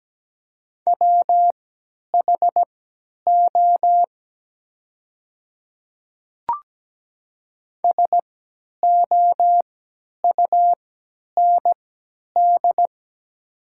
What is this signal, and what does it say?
Morse code, heard as a keyed tone, about 17 wpm, spaced as for 10 wpm, 700 Hz. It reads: WHO SOUND
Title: 3x Character Spacing - Top 300 Words - 17wpm